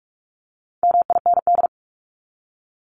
Morse code, heard as a keyed tone, about 44 wpm, 700 Hz. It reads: MIND
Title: Top 500 Words Repeated 45wpm